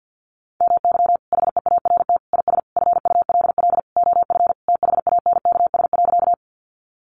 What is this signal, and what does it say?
Morse code, heard as a keyed tone, about 50 wpm, 700 Hz. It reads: MY HEART IS FULL OF THANKS.
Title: Sentences from Top 1200 Words 50wpm